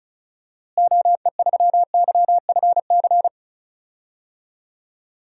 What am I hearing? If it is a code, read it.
Morse code, heard as a keyed tone, about 34 wpm, 700 Hz. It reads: OE3YFC
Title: DXCC-100 Call Signs - 6 Characters - Repeated 2x - 35wpm